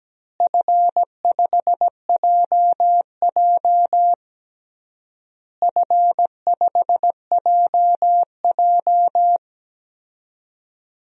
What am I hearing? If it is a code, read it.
Morse code, heard as a keyed tone, about 17 wpm, 700 Hz. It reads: F5JJ F5JJ
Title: DXCC-100 Call Signs - 4 Characters - Repeated 3x - 17wpm